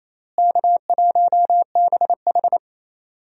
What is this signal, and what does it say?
Morse code, heard as a keyed tone, about 28 wpm, 700 Hz. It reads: K1BH
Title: USA Call Signs - 4 Characters - Repeated 3x - 28wpm